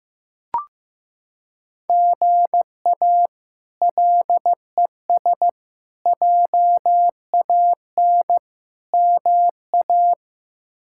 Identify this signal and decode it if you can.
Morse code, heard as a keyed tone, about 16 wpm, 700 Hz. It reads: GA LES JAN MA
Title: K1USN SST QSO Practice - 15wpm